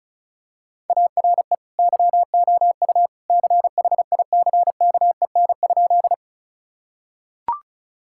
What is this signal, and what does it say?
Morse code, heard as a keyed tone, about 36 wpm, 700 Hz. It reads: ARE YOU CHICKEN?
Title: Sentences from Top 1000 Words - Review Part 5 - 35wpm